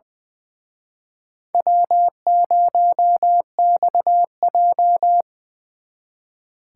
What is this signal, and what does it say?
Morse code, heard as a keyed tone, about 24 wpm, 700 Hz. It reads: W0XJ
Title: Speed Racing - USA 1x2 Call Signs 20wpm